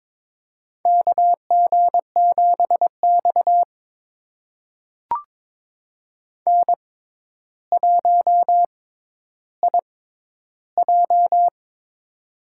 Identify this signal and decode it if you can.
Morse code, heard as a keyed tone, about 22 wpm, spaced as for 11 wpm, 700 Hz. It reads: KG7X N1IJ
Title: USA Call Signs - 4 Characters - 6x Character Spacing - 22wpm